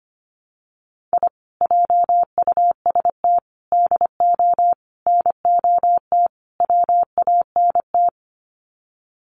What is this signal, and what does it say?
Morse code, heard as a keyed tone, about 25 wpm, 700 Hz. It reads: I JUST DO NOT WANT
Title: Sets of 5 Words Encore2 25wpm